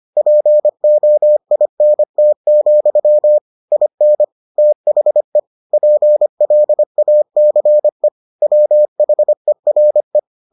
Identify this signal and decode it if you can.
Morse code, heard as a keyed tone, about 25 wpm, 600 Hz. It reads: POINT, IN THE PLACE WHERE